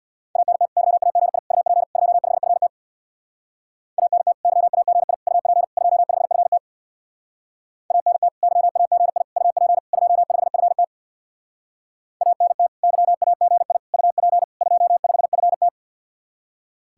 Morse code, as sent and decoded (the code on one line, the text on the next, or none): .- -. - / -.-- .- --. .. / ..- .--. / ..--- ..... ..-. - / .- -. - / -.-- .- --. .. / ..- .--. / ..--- ..... ..-. - / .- -. - / -.-- .- --. .. / ..- .--. / ..--- ..... ..-. - / .- -. - / -.-- .- --. .. / ..- .--. / ..--- ..... ..-. -
ANT YAGI UP 25FT ANT YAGI UP 25FT ANT YAGI UP 25FT ANT YAGI UP 25FT